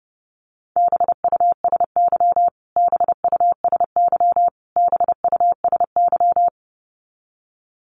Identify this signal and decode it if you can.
Morse code, heard as a keyed tone, about 30 wpm, 700 Hz. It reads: BUSY BUSY BUSY
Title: Top 500 Words Repeated 30wpm